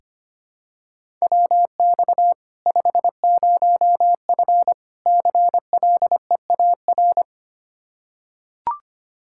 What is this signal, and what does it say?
Morse code, heard as a keyed tone, about 25 wpm, 700 Hz. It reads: WX 50F CLEAR
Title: K1USN SST QSO Practice 25wpm